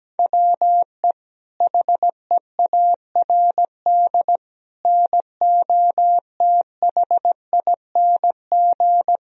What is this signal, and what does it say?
Morse code, heard as a keyed tone, about 17 wpm, 700 Hz. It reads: WE HEARD NOTHING
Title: Sentences from Top 1000 Words - Review Part 3 - 17wpm